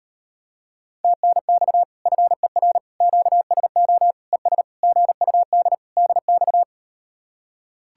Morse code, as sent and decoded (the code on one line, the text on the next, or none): - -. -..- / ..-. .-. / --.- ... --- / . ... / --. ..- -.. / -.. -..-
TNX FER QSO ES GUD DX